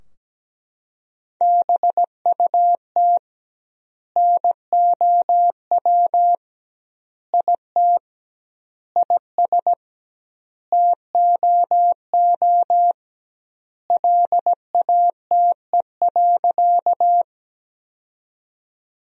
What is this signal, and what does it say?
Morse code, heard as a keyed tone, about 17 wpm, 700 Hz. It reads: BUT NOW IT IS TOO LATE.
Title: 2x Word-Spacing - Sentences from Top 300 Words - 17wpm